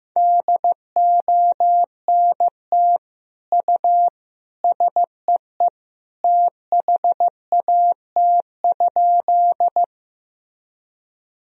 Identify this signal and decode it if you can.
Morse code, heard as a keyed tone, about 15 wpm, 700 Hz. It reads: DONT U SEE THAT?
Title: Sentences from Top 300 Words and Abbreviations 15wpm